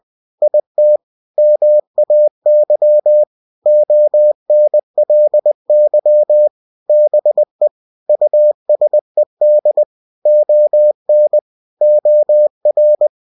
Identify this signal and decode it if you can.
Morse code, heard as a keyed tone, about 20 wpm, 600 Hz. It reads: IT MAY ONLY BE USED ON OR
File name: cw-books / winnie-the-pooh / winnie-20wpm-page-0113